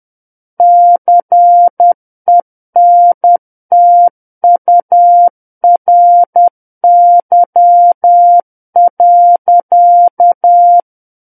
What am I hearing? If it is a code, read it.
Morse code, heard as a keyed tone, about 10 wpm, 700 Hz. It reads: CENTURY.